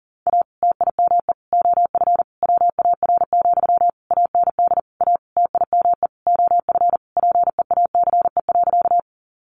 Morse code, heard as a keyed tone, about 40 wpm, 700 Hz. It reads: A TIME OF WAR, AND A TIME OF PEACE.
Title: Sentences from Top 1500 Words 40wpm